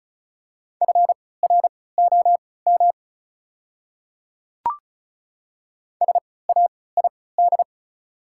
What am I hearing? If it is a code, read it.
Morse code, heard as a keyed tone, about 34 wpm, spaced as for 20 wpm, 700 Hz. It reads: FROM SAID
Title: 3x Character Spacing - Top 300 Words - 35wpm